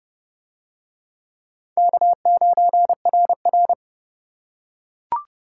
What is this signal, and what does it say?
Morse code, heard as a keyed tone, about 28 wpm, 700 Hz. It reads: K9RR